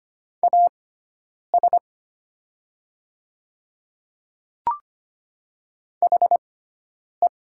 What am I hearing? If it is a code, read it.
Morse code, heard as a keyed tone, about 25 wpm, spaced as for 9 wpm, 700 Hz. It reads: AS HE
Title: 6x Character Spacing - Top 300 Words - 25wpm